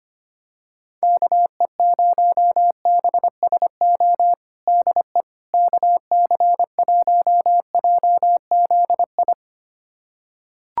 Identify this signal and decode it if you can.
Morse code, heard as a keyed tone, about 25 wpm, 700 Hz. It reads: KE0BSO DE KC1JZI
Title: USA Callsign DE Callsign - 6 Characters - 25wpm